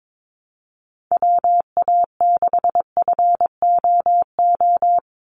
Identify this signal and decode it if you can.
Morse code, heard as a keyed tone, about 22 wpm, 700 Hz. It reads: WA6FOO
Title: USA Call Signs 22wpm